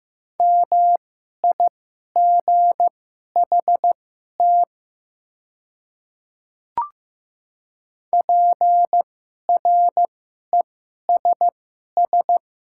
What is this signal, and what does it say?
Morse code, heard as a keyed tone, about 15 wpm, spaced as for 11 wpm, 700 Hz. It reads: MIGHT PRESS
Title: 2x Character Spacing - Top 300 Words - 15wpm